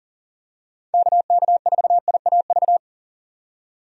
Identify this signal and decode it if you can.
Morse code, heard as a keyed tone, about 40 wpm, 700 Hz. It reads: KK4IAV
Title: USA Call Signs - 6 Characters - 40wpm